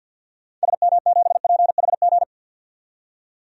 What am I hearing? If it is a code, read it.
Morse code, heard as a keyed tone, about 46 wpm, 700 Hz. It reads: SM7PHG
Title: DXCC-100 Call Signs - 6 Characters - 50wpm